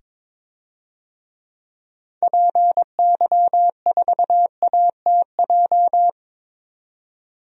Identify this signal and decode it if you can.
Morse code, heard as a keyed tone, about 22 wpm, 700 Hz. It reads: PY4ATJ
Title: DXCC-100 Call Signs - 6 Characters - 22wpm